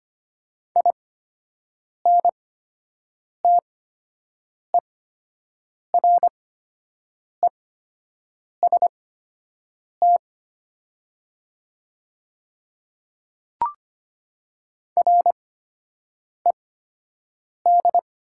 Morse code, heard as a keyed tone, about 25 wpm, spaced as for 7 wpm, 700 Hz. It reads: INTEREST RED
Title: 8x Character Spacing - Top 500 Words - 25wpm